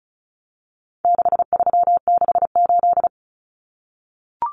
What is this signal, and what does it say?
Morse code, heard as a keyed tone, about 34 wpm, 700 Hz. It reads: 6368
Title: Common Cut Numbers - 4-Digit 35wpm